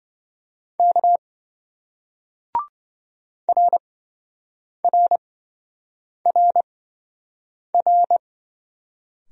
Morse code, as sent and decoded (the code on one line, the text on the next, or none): -.- / .-. .-. .-. .-.
K RRRR